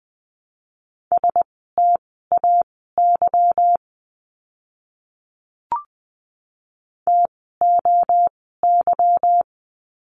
Morse code, heard as a keyed tone, about 20 wpm, spaced as for 15 wpm, 700 Hz. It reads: STAY TOY
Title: 146 Lesson 24f - Letter y - Words 20wpm